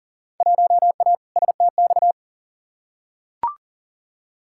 Morse code, heard as a keyed tone, about 40 wpm, 700 Hz. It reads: 1A STX
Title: Field Day Exchanges Encore - 40wpm